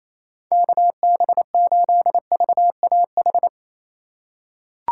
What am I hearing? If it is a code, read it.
Morse code, heard as a keyed tone, about 28 wpm, 700 Hz. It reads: KB8VAH
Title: USA Call Signs - 6 Characters - 4x Character Spacing - 28wpm